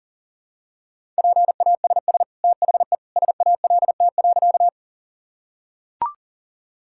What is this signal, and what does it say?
Morse code, heard as a keyed tone, about 40 wpm, 700 Hz. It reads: PASS THE SALT.
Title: Sentences from Top 1000 Words - Review Part 5 - 40wpm